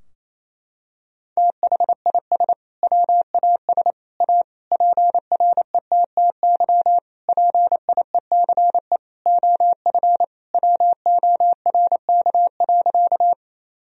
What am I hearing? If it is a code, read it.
Morse code, heard as a keyed tone, about 28 wpm, 700 Hz. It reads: THIS WAS A PRETTY PIECE OF WORK.
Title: Sentences from Top 1000 Words - Review Part 2 - 28wpm